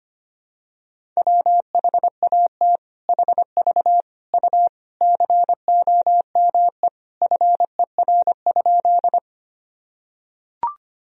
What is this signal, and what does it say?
Morse code, heard as a keyed tone, about 25 wpm, 700 Hz. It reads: WHAT HV U COME FER?
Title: Sentences from Top 100 Words and Abbreviations 25wpm